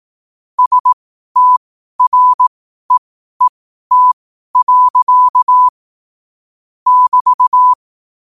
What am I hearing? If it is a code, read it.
Morse code, heard as a keyed tone, about 18 wpm, spaced as for 12 wpm, 1000 Hz. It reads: STREET. <BT>
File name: qotdfiles / 12wpm